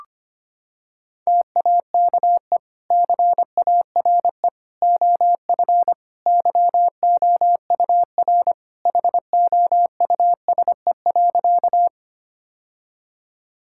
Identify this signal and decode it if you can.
Morse code, heard as a keyed tone, about 25 wpm, 700 Hz. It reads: TAKE CARE OF YOUR HOUSE.
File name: practice / 25wpm / Sentences with 2-Word Phrases - 25wpm